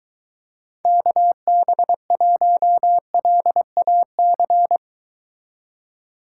Morse code, heard as a keyed tone, about 22 wpm, 700 Hz. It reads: KB1LAC